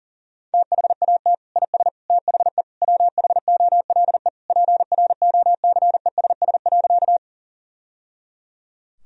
Morse code, heard as a keyed tone, about 40 wpm, 700 Hz. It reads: THAT IS THE WHOLE PROCESS.
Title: Sentences from Top 1000 Words - Review Part 3 - 40wpm